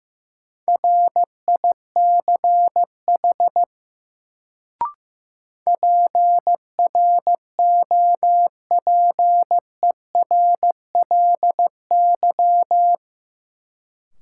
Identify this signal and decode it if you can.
Morse code, heard as a keyed tone, about 15 wpm, 700 Hz. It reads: RICH PROPERLY
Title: Sentences from Top 1100 Words - New Word Drill - 15wpm